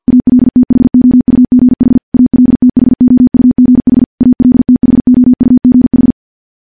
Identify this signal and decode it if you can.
Morse code, heard as a keyed tone, about 50 wpm, 250 Hz. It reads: ALTHOUGH ALTHOUGH ALTHOUGH
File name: morse / morse-training-50WPM